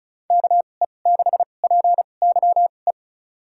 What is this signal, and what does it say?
Morse code, heard as a keyed tone, about 36 wpm, spaced as for 26 wpm, 700 Hz. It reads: KE6PYE
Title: USA Call Signs - 2x Character Spacing - 35wpm